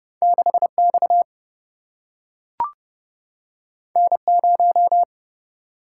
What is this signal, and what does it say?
Morse code, heard as a keyed tone, about 30 wpm, 700 Hz. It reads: 6 X N 0
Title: Single-Letter Single-Number 30wpm